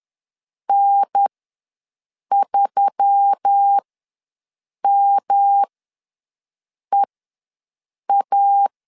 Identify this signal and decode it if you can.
Morse code, heard as a keyed tone, about 10 wpm, spaced as for 6 wpm, 800 Hz. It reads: N3MEA
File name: LEZIONI / 7 GF9Z / 7 GF9 ws 10wpm 4034